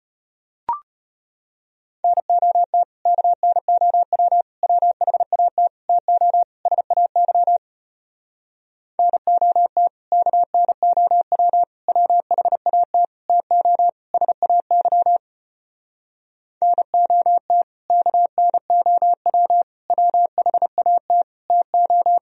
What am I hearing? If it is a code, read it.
Morse code, heard as a keyed tone, about 33 wpm, 700 Hz. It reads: NOT KNOW WHAT TO SAY NOT KNOW WHAT TO SAY NOT KNOW WHAT TO